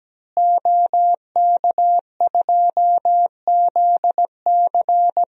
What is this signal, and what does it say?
Morse code, heard as a keyed tone, about 17 wpm, 700 Hz. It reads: OK2ZC
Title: DXCC-100 Call Signs - 5 Characters - Repeated 2x - 17wpm